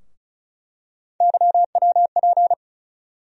Morse code, heard as a keyed tone, about 36 wpm, 700 Hz. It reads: YWP
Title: USA Call Sign Postfixes - 3 Letters 35wpm